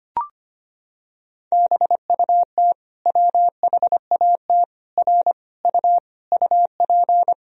Morse code, heard as a keyed tone, about 25 wpm, 700 Hz. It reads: BUT WHAT R U UP
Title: Sentences from Top 100 Words and Abbreviations 25wpm